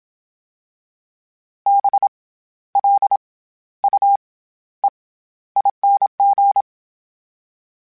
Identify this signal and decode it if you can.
Morse code, heard as a keyed tone, about 27 wpm, 800 Hz. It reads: B L U E ING
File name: Special INGs-mp3s / Special INGs-mp3s / SlowToFaster ing @25